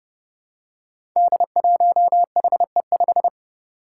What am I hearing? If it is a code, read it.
Morse code, heard as a keyed tone, about 28 wpm, 700 Hz. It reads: D1HE5
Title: Five Challenging Letter-Number Combinations 30wpm